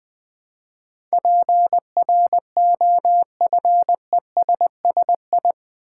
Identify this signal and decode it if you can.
Morse code, heard as a keyed tone, about 20 wpm, 700 Hz. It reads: PROFESSI
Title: Build A Word - 4 Letters At A Time - 12-Letter Words - 20wpm